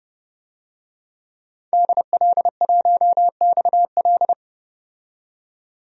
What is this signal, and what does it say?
Morse code, heard as a keyed tone, about 29 wpm, 700 Hz. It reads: DL1XL
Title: DXCC-100 Call Signs - 5 Characters - 30wpm